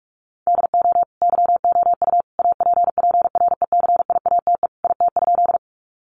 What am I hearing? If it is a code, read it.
Morse code, heard as a keyed tone, about 44 wpm, 700 Hz. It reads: DO YOU APPRECIATE IT?
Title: Sentences from Top 1300 Words 45wpm